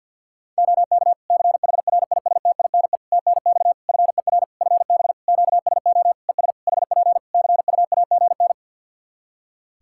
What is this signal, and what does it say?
Morse code, heard as a keyed tone, about 51 wpm, 700 Hz. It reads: OK CHRISTINE TNX FER FB QSO ES HP CUAGN
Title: Practice QSO - Exchange 4 - Ending - 50wpm